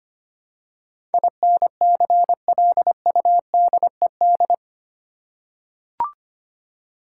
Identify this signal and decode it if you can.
Morse code, heard as a keyed tone, about 25 wpm, 700 Hz. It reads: INCLUDED
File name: Eight Letter Words - 25wpm